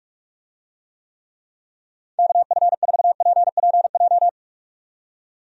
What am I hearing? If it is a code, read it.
Morse code, heard as a keyed tone, about 46 wpm, 700 Hz. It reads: KF4PPJ
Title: USA Call Signs - 6 Characters - 45wpm